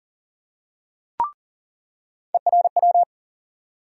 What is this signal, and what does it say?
Morse code, heard as a keyed tone, about 40 wpm, 700 Hz. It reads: ERW
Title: USA Call Sign Postfixes - 3 Letters 40wpm